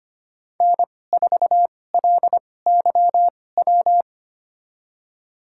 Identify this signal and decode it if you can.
Morse code, heard as a keyed tone, about 25 wpm, spaced as for 18 wpm, 700 Hz. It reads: N4LYW
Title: USA Call Signs - 5 Characters - 2x Character Spacing - 25wpm